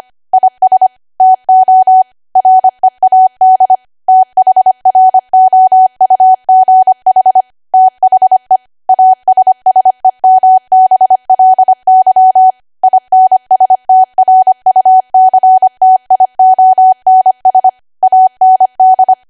Morse code, heard as a keyed tone, about 25 wpm, 750 Hz. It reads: IS TO READ THROUGH THE ASSEMBLY INSTRUCTIONS AND